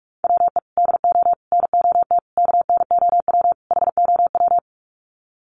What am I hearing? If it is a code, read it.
Morse code, heard as a keyed tone, about 44 wpm, 700 Hz. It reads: WE DO NOT KNOW HOW
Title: Sets of 5 Words Encore 45wpm